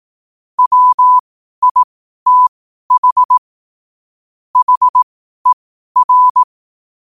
Morse code, heard as a keyed tone, about 18 wpm, spaced as for 12 wpm, 1000 Hz. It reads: WITH HER
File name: qotdfiles / 12wpm